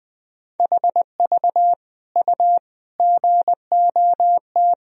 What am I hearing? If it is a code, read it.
Morse code, heard as a keyed tone, about 20 wpm, 700 Hz. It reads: HV U GOT
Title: Sentences from Top 300 Words and Abbreviations 20wpm